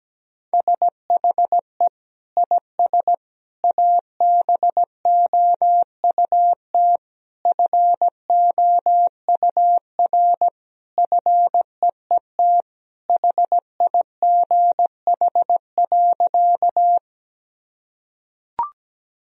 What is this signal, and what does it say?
Morse code, heard as a keyed tone, about 17 wpm, 700 Hz. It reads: SHE IS ABOUT FOUR FEET HIGH.
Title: Sentences from Top 1000 Words - Review Part 1 - 17wpm